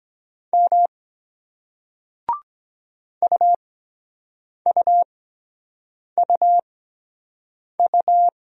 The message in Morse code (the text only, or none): M UUUU